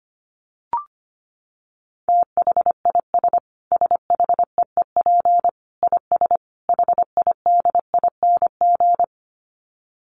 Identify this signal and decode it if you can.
Morse code, heard as a keyed tone, about 25 wpm, 700 Hz. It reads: THIS SHEEP IS HIDING
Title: Dit Challenge 25wpm